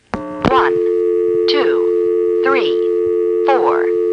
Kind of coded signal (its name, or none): telephone call progress tone